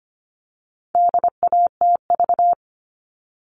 Morse code, heard as a keyed tone, about 25 wpm, 700 Hz. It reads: DATV
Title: All Cut Numbers - 4-Digit 25wpm